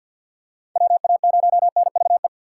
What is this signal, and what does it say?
Morse code, heard as a keyed tone, about 48 wpm, 700 Hz. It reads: WA0NVE
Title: USA Call Signs - 6 Characters - 3x Character Spacing - 50wpm